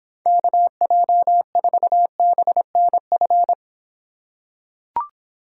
Morse code, heard as a keyed tone, about 24 wpm, 700 Hz. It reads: KJ4BNF